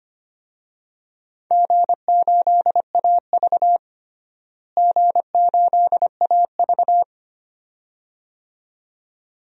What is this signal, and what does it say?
Morse code, heard as a keyed tone, about 25 wpm, 700 Hz. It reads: G8AV G8AV
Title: DXCC-100 Call Signs - 4 Characters - Repeated 2x - 25wpm